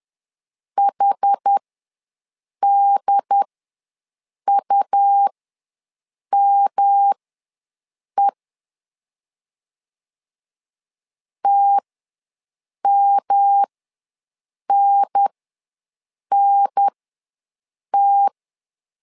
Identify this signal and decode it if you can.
Morse code, heard as a keyed tone, about 10 wpm, spaced as for 6 wpm, 800 Hz. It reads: HDUME TMNNT
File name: LEZIONI / 3 MWRO / 3 M ws 10wpm 2121